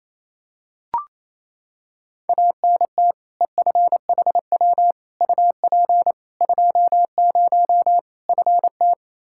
Morse code, heard as a keyed tone, about 28 wpm, 700 Hz. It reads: ANT EFHW UP 20 FT
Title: Practice QSO - Exchange 2 - Info - 28wpm